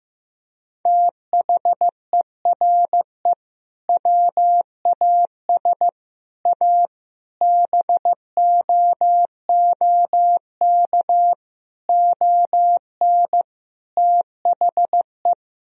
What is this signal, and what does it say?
Morse code, heard as a keyed tone, about 16 wpm, 700 Hz. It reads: THERE WAS A BOOK ON THE
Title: Sentences from Top 1000 Words - Review Part 3 - 15wpm